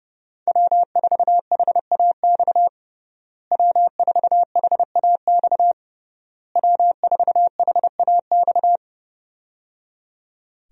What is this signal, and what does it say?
Morse code, heard as a keyed tone, about 30 wpm, 700 Hz. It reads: W4HAX W4HAX W4HAX
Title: USA Call Signs - 5 Characters - Repeated 3x - 30wpm